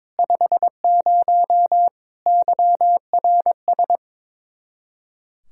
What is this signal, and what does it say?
Morse code, heard as a keyed tone, about 22 wpm, 700 Hz. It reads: 50 YRS
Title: Practice QSO - Exchange 3 - More Info - 22wpm